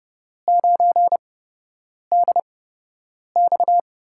Morse code, heard as a keyed tone, about 30 wpm, spaced as for 8 wpm, 700 Hz. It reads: 9DX
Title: USA Call Signs - 5 Characters - 8x Character Spacing - 30wpm